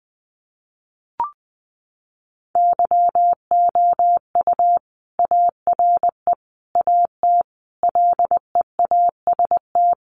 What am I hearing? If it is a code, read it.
Morse code, heard as a keyed tone, about 20 wpm, 700 Hz. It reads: YOU ARE AT LEAST
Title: Sentences from Top 1100 Words 20wpm